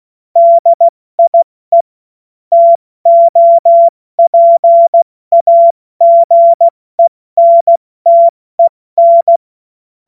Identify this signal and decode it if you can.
Morse code, heard as a keyed tone, about 16 wpm, spaced as for 14 wpm, 680 Hz. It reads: DIE TOPAGENTEN
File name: Morsekurs/Witze